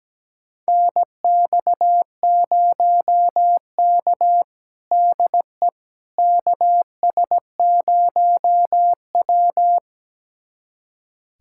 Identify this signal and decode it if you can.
Morse code, heard as a keyed tone, about 17 wpm, 700 Hz. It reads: NX0K DE KS0W
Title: USA Callsign DE Callsign - 4 Characters - 17wpm